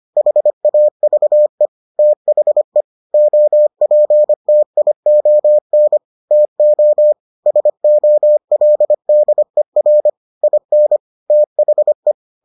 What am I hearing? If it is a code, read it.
Morse code, heard as a keyed tone, about 25 wpm, 600 Hz. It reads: HAVE THE OPTION TO SOLDER IN THE